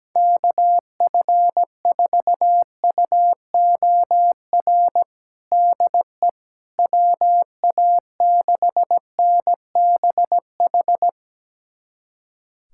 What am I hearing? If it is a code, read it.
Morse code, heard as a keyed tone, about 17 wpm, 700 Hz. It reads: KF4UOR DE WA6NBH
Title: USA Callsign DE Callsign - 6 Characters - 17wpm